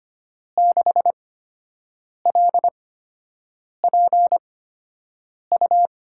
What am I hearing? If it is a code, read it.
Morse code, heard as a keyed tone, about 25 wpm, spaced as for 7 wpm, 700 Hz. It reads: 6LPU
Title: USA Call Signs - 5 Characters - 8x Character Spacing - 25wpm